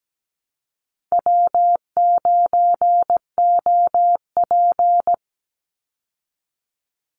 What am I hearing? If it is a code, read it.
Morse code, heard as a keyed tone, about 17 wpm, 700 Hz. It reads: W9OP
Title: Call Signs Repeated 3x - 17wpm